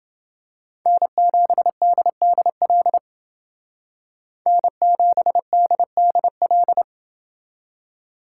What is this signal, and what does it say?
Morse code, heard as a keyed tone, about 28 wpm, 700 Hz. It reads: N7DDL N7DDL